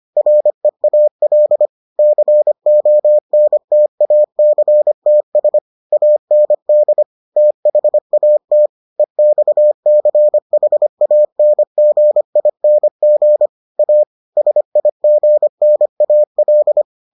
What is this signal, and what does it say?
Morse code, heard as a keyed tone, about 25 wpm, 600 Hz. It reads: REAL CONTACTS AND THAT EXCHANGING A SIGNAL